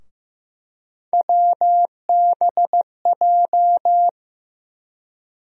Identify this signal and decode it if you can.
Morse code, heard as a keyed tone, about 16 wpm, 700 Hz. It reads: WBJ
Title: USA Call Sign Postfixes - 3 Letters 15wpm